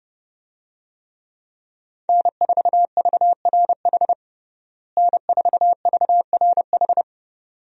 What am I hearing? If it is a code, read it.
Morse code, heard as a keyed tone, about 32 wpm, 700 Hz. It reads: N4VRH N4VRH
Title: USA Call Signs - 5 Characters - Repeated 2x - 30wpm